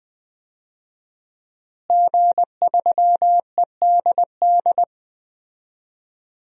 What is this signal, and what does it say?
Morse code, heard as a keyed tone, about 20 wpm, 700 Hz. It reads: G3EDD
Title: DXCC-100 Call Signs - 5 Characters - Repeated 3x - 20wpm